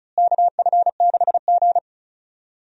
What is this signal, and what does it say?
Morse code, heard as a keyed tone, about 36 wpm, 700 Hz. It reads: KF6G
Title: USA Call Signs - 4 Characters - Repeated 3x - 35wpm